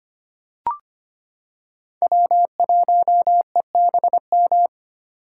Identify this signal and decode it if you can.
Morse code, heard as a keyed tone, about 25 wpm, 700 Hz. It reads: W1EBM
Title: K1USN SST QSO Practice - 25wpm